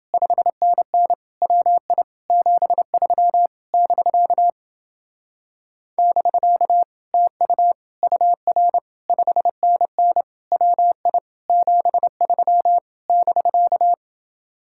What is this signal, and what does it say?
Morse code, heard as a keyed tone, about 29 wpm, 700 Hz. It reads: 5NN WI 73 <BK> <BK> TU UR 5NN WI 73 <BK>